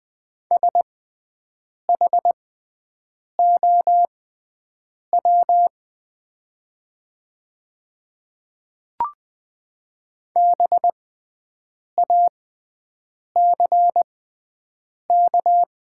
Morse code, heard as a keyed tone, about 20 wpm, spaced as for 7 wpm, 700 Hz. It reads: SHOW BACK
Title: 6x Character Spacing - Top 200 Words - 20wpm